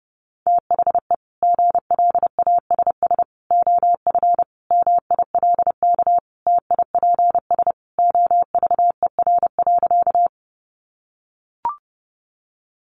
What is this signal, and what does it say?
Morse code, heard as a keyed tone, about 30 wpm, 700 Hz. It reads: THE GLASS OF MILK TIPS OVER.
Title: Additional Easy Phrases 30wpm